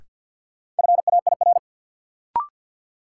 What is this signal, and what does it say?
Morse code, heard as a keyed tone, about 48 wpm, 700 Hz. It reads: FAIR
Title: Sentences from Top 1300 Words - New Word Drill Alt - 50wpm